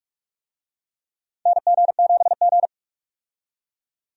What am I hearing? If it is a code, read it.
Morse code, heard as a keyed tone, about 45 wpm, 700 Hz. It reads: NG7G
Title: USA Call Signs - 4 Characters - 45wpm